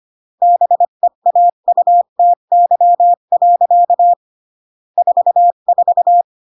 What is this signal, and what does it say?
Morse code, heard as a keyed tone, about 25 wpm, 700 Hz. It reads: BEAUTY. 44